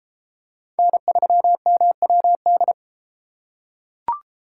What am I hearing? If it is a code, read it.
Morse code, heard as a keyed tone, about 32 wpm, 700 Hz. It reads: N3MWD